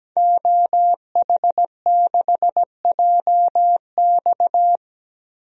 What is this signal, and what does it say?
Morse code, heard as a keyed tone, about 17 wpm, 700 Hz. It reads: OH6JX
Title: DXCC-100 Call Signs - 5 Characters - 17wpm